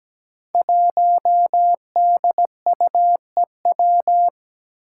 Morse code, heard as a keyed tone, about 17 wpm, 700 Hz. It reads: 1DUEW
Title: Five Challenging Letter-Number Combinations 17wpm